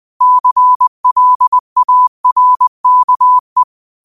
Morse code, heard as a keyed tone, about 20 wpm, 1000 Hz. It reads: CLARKE
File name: qotdfiles / 20wpm